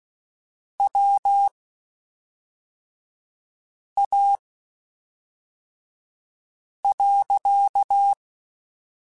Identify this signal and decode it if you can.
Morse code, heard as a keyed tone, about 16 wpm, spaced as for 3 wpm, 800 Hz. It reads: WA.
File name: (spaced) / QSOs Spaced-mp3s / kc7sac @5 sp